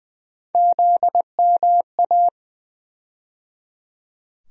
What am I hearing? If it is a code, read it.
Morse code, heard as a keyed tone, about 20 wpm, 700 Hz. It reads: ZMA
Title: USA Call Sign Postfixes - 3 Letters 20wpm